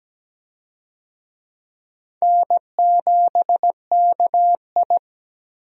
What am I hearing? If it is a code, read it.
Morse code, heard as a keyed tone, about 17 wpm, 700 Hz. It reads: N7KI
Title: USA Call Signs - 4 Characters - 17wpm